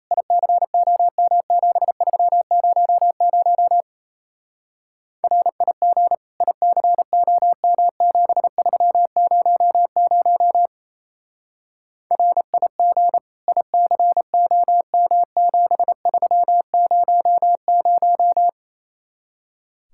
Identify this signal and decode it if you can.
Morse code, heard as a keyed tone, about 35 wpm, 700 Hz. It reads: ICOM7300 RIG ICOM7300 RIG ICOM7300
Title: Speed-Racing Common QSO Phrases Encore - 28wpm